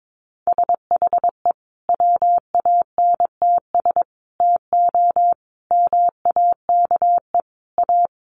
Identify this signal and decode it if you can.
Morse code, heard as a keyed tone, about 22 wpm, 700 Hz. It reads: SHE WANTS TO MAKE A
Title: Easy Phrases 22wpm